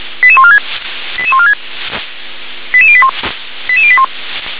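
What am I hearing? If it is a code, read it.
Five-tone selective call (ZVEI), heard as quick runs of five tones, tones of about 70 ms. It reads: 88146 88146 88081 88081
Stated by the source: Received on an SDR, ZVEI five-tone selective call.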